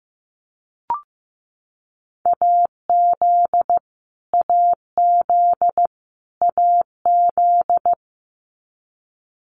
Morse code, heard as a keyed tone, about 15 wpm, 700 Hz. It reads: AZ AZ AZ
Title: US States Repeated 3x 15wpm